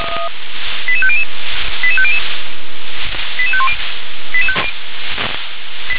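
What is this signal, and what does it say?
Five-tone selective call (ZVEI), heard as quick runs of five tones, tones of about 70 ms. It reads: 88500 88500 88510
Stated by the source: Five-tone ZVEI selective call 88753 received on SDR.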